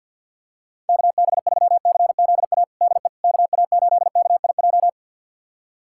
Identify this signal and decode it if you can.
Morse code, heard as a keyed tone, about 50 wpm, 690 Hz. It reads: KB3CZA DE KA8KIJ
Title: USA Callsign DE Callsign - 6 Characters - 50wpm